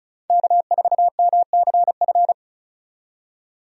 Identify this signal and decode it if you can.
Morse code, heard as a keyed tone, about 36 wpm, 700 Hz. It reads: K4MCF
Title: USA Call Signs - 5 Characters - 35wpm